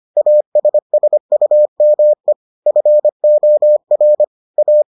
Morse code, heard as a keyed tone, about 25 wpm, 600 Hz. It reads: ASSUME FOR A